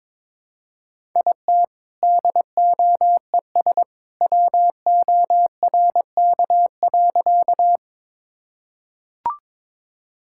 Morse code, heard as a keyed tone, about 22 wpm, 700 Hz. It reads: IT DOES WORK.
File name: practice / 22wpm / Sentences with 2-Word Phrases - 22wpm